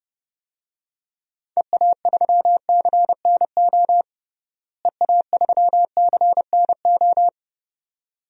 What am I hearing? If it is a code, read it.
Morse code, heard as a keyed tone, about 30 wpm, 700 Hz. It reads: EA3CNO EA3CNO
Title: DXCC-100 Call Signs - 6 Characters - Repeated 2x - 30wpm